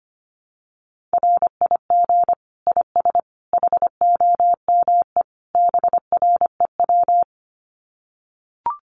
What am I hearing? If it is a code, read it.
Morse code, heard as a keyed tone, about 25 wpm, 700 Hz. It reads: RIG IS HOME BREW
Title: Common QSO Phrases 25wpm